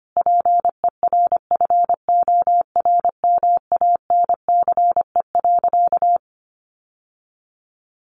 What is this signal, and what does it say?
Morse code, heard as a keyed tone, about 25 wpm, 700 Hz. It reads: PERFORMANCE.
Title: Sentences from Top 1100 Words 25wpm